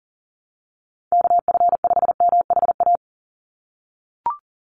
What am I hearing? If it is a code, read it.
Morse code, heard as a keyed tone, about 38 wpm, 700 Hz. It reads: KF5MHA
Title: USA Call Signs 40wpm